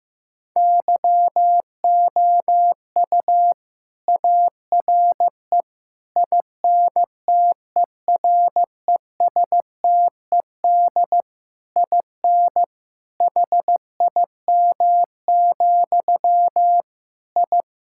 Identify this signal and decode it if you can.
Morse code, heard as a keyed tone, about 15 wpm, 700 Hz. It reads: YOU ARE INTERESTED IN HIM, I
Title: Sentences from Top 1000 Words - Review Part 6 - 15wpm